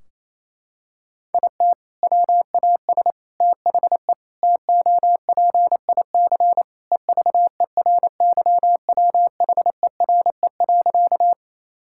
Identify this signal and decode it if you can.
Morse code, heard as a keyed tone, about 28 wpm, 700 Hz. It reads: IT WAS THE TOPIC EVERYWHERE.
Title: Sentences from Top 1600 Words 28wpm